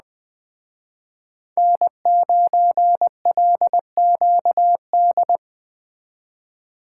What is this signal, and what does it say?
Morse code, heard as a keyed tone, about 20 wpm, 700 Hz. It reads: N9LQD
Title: Speed Racing - USA 1x3 Call Signs 17wpm